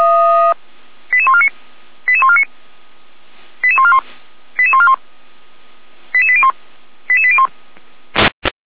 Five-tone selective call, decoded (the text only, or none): 88149 88149 88141 88141 88801 88801